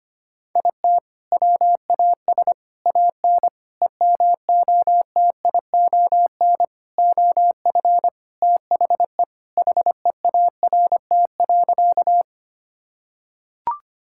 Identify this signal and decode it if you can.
Morse code, heard as a keyed tone, about 25 wpm, 700 Hz. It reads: IT WAS AN EMOTION OF THE HEART.
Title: Sentences from Top 1600 Words 25wpm